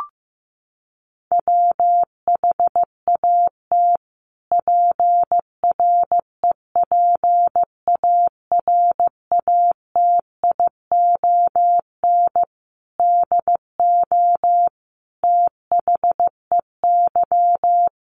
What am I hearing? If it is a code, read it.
Morse code, heard as a keyed tone, about 14 wpm, 700 Hz. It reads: WHAT PREPARATION DO THEY
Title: Sentences from Top 1400 Words 15wpm